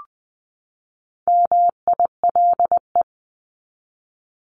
Morse code, heard as a keyed tone, about 20 wpm, 700 Hz. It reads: MILE
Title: ES:S Suffix - Top 100 Words with Base Words 20wpm